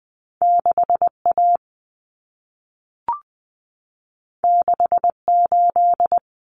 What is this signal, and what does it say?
Morse code, heard as a keyed tone, about 20 wpm, 700 Hz. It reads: 6A 68
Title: Common Cut Numbers - 2-Digit 20wpm